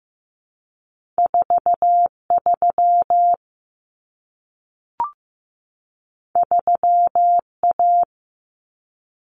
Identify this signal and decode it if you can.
Morse code, heard as a keyed tone, about 15 wpm, 700 Hz. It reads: 43 3A
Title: Common Cut Numbers - 2-Digit 15wpm